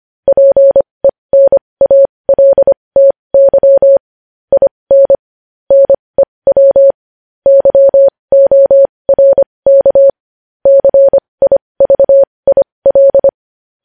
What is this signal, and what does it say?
Morse code, heard as a keyed tone, about 25 wpm, spaced as for 20 wpm, 550 Hz. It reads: PENALTY IN NEW YORK CIVIL
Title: News Headlines at 20